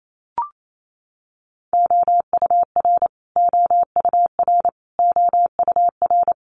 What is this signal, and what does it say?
Morse code, heard as a keyed tone, about 28 wpm, 700 Hz. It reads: OUR OUR OUR
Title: Top 200 Words Repeated 28wpm